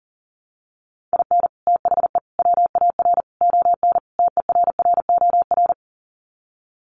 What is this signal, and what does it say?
Morse code, heard as a keyed tone, about 40 wpm, 700 Hz. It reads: IN THE WAR ON TERROR
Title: Sets of 5 Words Encore2 40wpm